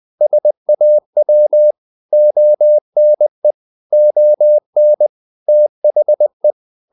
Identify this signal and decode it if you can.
Morse code, heard as a keyed tone, about 20 wpm, 600 Hz. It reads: SAW ONE ON THE
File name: the old man and the sea part 2 20wpm